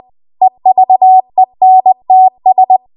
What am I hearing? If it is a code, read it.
Morse code, heard as a keyed tone, about 20 wpm, 750 Hz. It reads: EVENTS